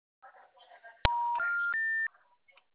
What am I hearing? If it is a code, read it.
Special information tone.
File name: out- -2016- - - .50439